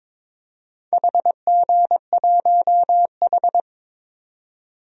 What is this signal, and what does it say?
Morse code, heard as a keyed tone, about 22 wpm, 700 Hz. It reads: HG1H
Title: DXCC-100 Call Signs - 4 Characters - 22wpm